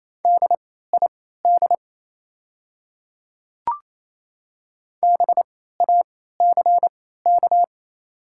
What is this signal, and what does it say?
Morse code, heard as a keyed tone, about 28 wpm, spaced as for 16 wpm, 700 Hz. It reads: DID BACK
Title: 3x Character Spacing - Top 200 Words - 28wpm